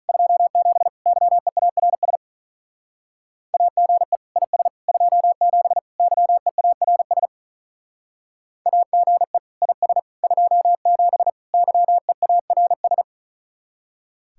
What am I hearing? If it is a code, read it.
Morse code, heard as a keyed tone, about 46 wpm, 700 Hz. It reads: 27 YEARS AGE IS 27 YEARS AGE IS 27 YEARS